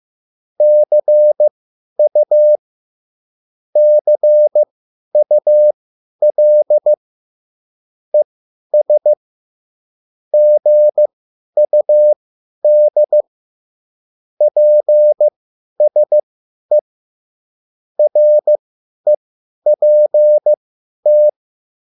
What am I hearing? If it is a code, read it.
Morse code, heard as a keyed tone, about 15 wpm, spaced as for 10 wpm, 600 Hz. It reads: CU CUL ES GUD PSE REPT